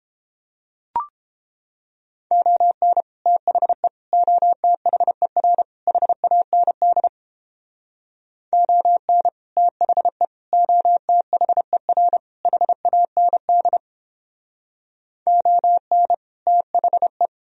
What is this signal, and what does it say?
Morse code, heard as a keyed tone, about 29 wpm, 700 Hz. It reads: ON THE OTHER HAND ON THE OTHER HAND ON THE